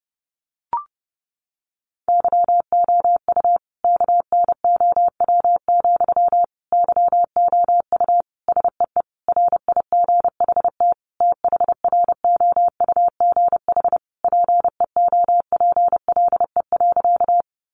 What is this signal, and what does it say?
Morse code, heard as a keyed tone, about 30 wpm, 700 Hz. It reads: YOU KNOW, YOU SEE RIGHT THROUGH PEOPLE.
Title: Sentences from Top 200 Words 30wpm